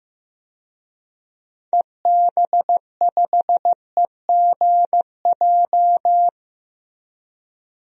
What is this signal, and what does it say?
Morse code, heard as a keyed tone, about 15 wpm, 700 Hz. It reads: EB5EGJ
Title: DXCC-100 Call Signs - 6 Characters - Repeated 2x - 15wpm